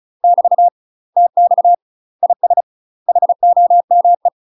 Morse code, heard as a keyed tone, about 34 wpm, 700 Hz. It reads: <BT> TX IS HOME
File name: UK MASTER QSOs TRAINING / 35WPM / UK MASTER QSOs 0025